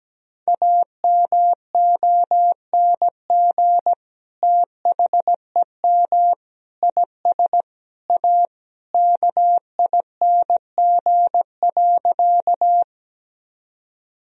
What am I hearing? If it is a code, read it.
Morse code, heard as a keyed tone, about 17 wpm, 700 Hz. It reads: AMONG THEM IS A KING.
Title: Sentences from Top 1000 Words - Review Part 3 - 17wpm